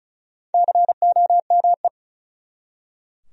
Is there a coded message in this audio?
Yes: Morse code from 0.5 to 1.9 s, heard as a keyed tone, about 46 wpm, 700 Hz. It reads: COME